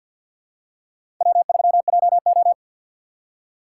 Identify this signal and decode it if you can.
Morse code, heard as a keyed tone, about 53 wpm, 700 Hz. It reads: W3JO
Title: USA Call Signs - 4 Characters - 2x Character Spacing - 50wpm